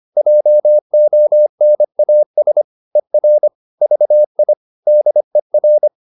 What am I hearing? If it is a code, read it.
Morse code, heard as a keyed tone, about 25 wpm, 600 Hz. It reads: JONAS ER VI DER
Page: Ferie 25 wpm